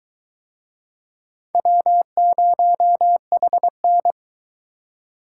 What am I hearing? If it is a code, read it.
Morse code, heard as a keyed tone, about 22 wpm, 700 Hz. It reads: W0HN